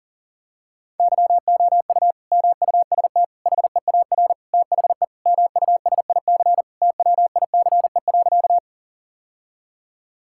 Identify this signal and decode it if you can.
Morse code, heard as a keyed tone, about 40 wpm, 700 Hz. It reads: YOU MUST HEAR THE MUSIC TWICE.
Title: Sentences from Top 1600 Words 40wpm